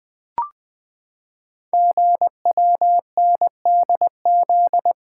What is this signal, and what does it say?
Morse code, heard as a keyed tone, about 20 wpm, 700 Hz. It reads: GWNDZ
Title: Code Groups - 5 Letters - 20wpm